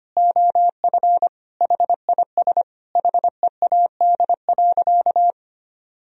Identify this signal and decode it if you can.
Morse code, heard as a keyed tone, about 25 wpm, 700 Hz. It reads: OF HIS HEAD.
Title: Sentences from Top 1000 Words - Review Part 2 - 25wpm